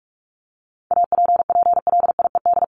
Morse code, heard as a keyed tone, about 44 wpm, 700 Hz. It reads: APPLIED
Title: ED Suffix - Top 100 Words with Base Words 45wpm